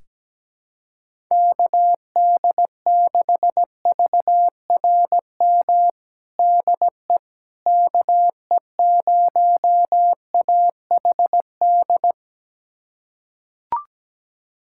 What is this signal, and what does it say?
Morse code, heard as a keyed tone, about 17 wpm, 700 Hz. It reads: KD6VRM DE KE0AHD
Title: USA Callsign DE Callsign 17wpm